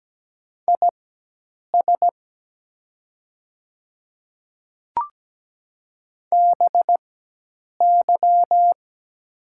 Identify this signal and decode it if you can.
Morse code, heard as a keyed tone, about 17 wpm, spaced as for 8 wpm, 700 Hz. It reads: IS BY
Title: 4x Character Spacing - Top 100 Words - 17wpm